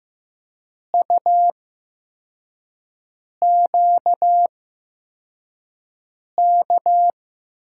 Morse code, heard as a keyed tone, about 15 wpm, spaced as for 4 wpm, 700 Hz. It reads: UQK
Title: USA Call Signs - 5 Characters - 8x Character Spacing - 15wpm